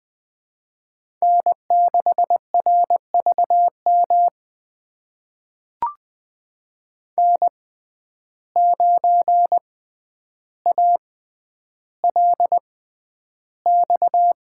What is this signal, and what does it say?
Morse code, heard as a keyed tone, about 20 wpm, spaced as for 10 wpm, 700 Hz. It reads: N6RVM N9ALX
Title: USA Call Signs - 5 Characters - 6x Character Spacing - 20wpm